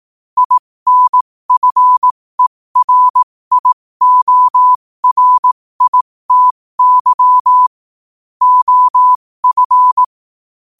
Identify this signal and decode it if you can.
Morse code, heard as a keyed tone, about 18 wpm, spaced as for 15 wpm, 1000 Hz. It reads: INFERIORITY OF
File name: qotdfiles / 15wpm